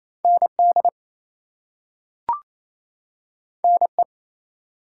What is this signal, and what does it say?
Morse code, heard as a keyed tone, about 28 wpm, 700 Hz. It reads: ND NE
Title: US States and Canadian Provinces 28wpm